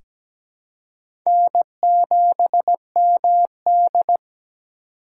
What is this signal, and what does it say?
Morse code, heard as a keyed tone, about 17 wpm, 700 Hz. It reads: N7MD
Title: USA Call Signs - 4 Characters - 4x Character Spacing - 17wpm